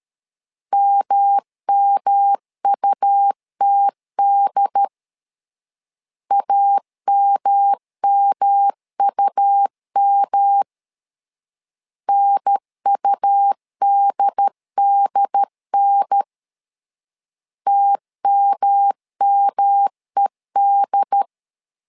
Morse code, heard as a keyed tone, about 13 wpm, 800 Hz. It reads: MMUTD AMMUM NUDDN TMMED
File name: LEZIONI / 3 MWRO / 3 M 12wpm 1541